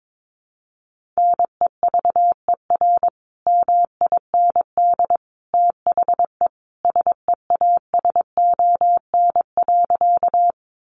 Morse code, heard as a keyed tone, about 22 wpm, 700 Hz. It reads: NEVER MIND THE SEASON.
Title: Sentences from Top 1400 Words 22wpm